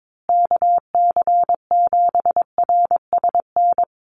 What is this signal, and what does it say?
Morse code, heard as a keyed tone, about 22 wpm, 700 Hz. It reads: KC7RSN
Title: USA Call Signs Repeated 2x 22wpm